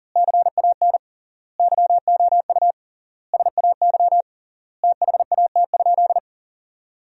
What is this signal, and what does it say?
Morse code, heard as a keyed tone, about 40 wpm, 700 Hz. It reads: CAN YOU SAY THAT?
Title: 3x Word-Spacing - Sentences from Top 200 Words - 40wpm